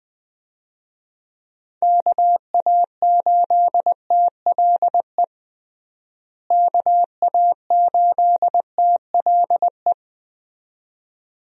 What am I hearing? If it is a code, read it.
Morse code, heard as a keyed tone, about 20 wpm, 700 Hz. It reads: KA8TLE KA8TLE
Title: USA Call Signs - 6 Characters - Repeated 2x - 20wpm